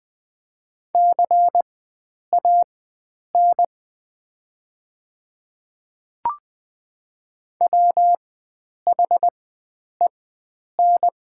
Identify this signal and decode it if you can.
Morse code, heard as a keyed tone, about 20 wpm, spaced as for 9 wpm, 700 Hz. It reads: CAN WHEN
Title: 4x Character Spacing - Top 300 Words - 20wpm